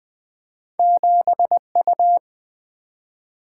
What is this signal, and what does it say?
Morse code, heard as a keyed tone, about 21 wpm, 700 Hz. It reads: 7U
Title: Single-Letter Single-Number 20wpm